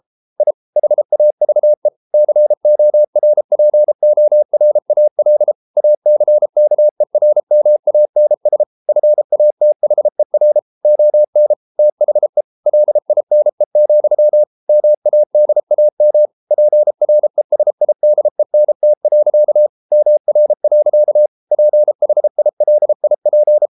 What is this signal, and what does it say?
Morse code, heard as a keyed tone, about 33 wpm, 600 Hz. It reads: I HAVE CORPORAL ACKERMANS FATHER ON THE LINE, MADAM PRESIDENT. MR. PHILIP